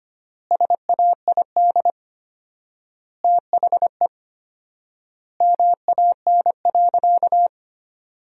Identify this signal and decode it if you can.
Morse code, heard as a keyed tone, about 25 wpm, 700 Hz. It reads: SAID THE MAN.
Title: 4x Word-Spacing - Sentences from Top 200 Words - 25wpm